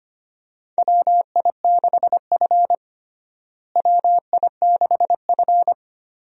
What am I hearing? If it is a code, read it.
Morse code, heard as a keyed tone, about 25 wpm, 700 Hz. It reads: WI6F WI6F
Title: USA Call Signs - 4 Characters - Repeated 2x - 25wpm